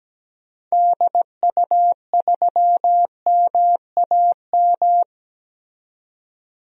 Morse code, heard as a keyed tone, about 17 wpm, 700 Hz. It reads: DU3MAM
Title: DXCC-100 Call Signs - 6 Characters - 17wpm